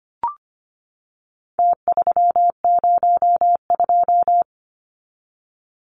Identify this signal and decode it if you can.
Morse code, heard as a keyed tone, about 25 wpm, 700 Hz. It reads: T302
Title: Numbers - Contest Style - 4-Digit 25wpm